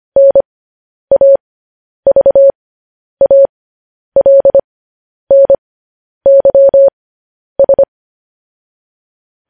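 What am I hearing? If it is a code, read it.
Morse code, heard as a keyed tone, about 25 wpm, spaced as for 10 wpm, 550 Hz. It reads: NAVALNYS
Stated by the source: Morse code sending of news headlines, speed 10.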